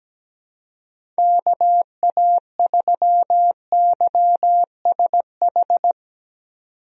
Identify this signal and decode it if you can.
Morse code, heard as a keyed tone, about 17 wpm, 700 Hz. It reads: KA3YSH
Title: USA Call Signs - 6 Characters - 17wpm